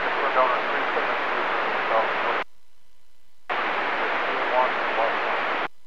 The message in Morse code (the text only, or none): none